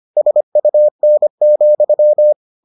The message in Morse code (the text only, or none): SUN,